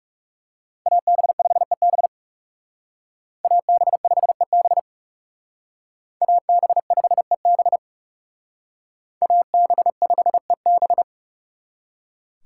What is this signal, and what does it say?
Morse code, heard as a keyed tone, about 38 wpm, 700 Hz. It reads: AB5EB AB5EB AB5EB AB5EB